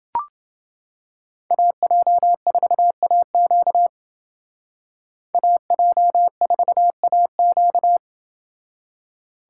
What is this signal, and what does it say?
Morse code, heard as a keyed tone, about 28 wpm, 700 Hz. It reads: AJ4AQ AJ4AQ